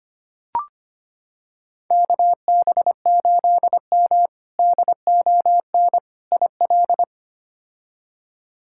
Morse code, heard as a keyed tone, about 25 wpm, 700 Hz. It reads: KB8M DON IL
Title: K1USN SST QSO Practice - 25wpm